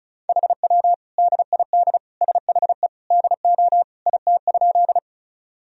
Morse code, heard as a keyed tone, about 34 wpm, 700 Hz. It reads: HW DID SHE DO IT?
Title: Sentences from Top 100 Words and Abbreviations 35wpm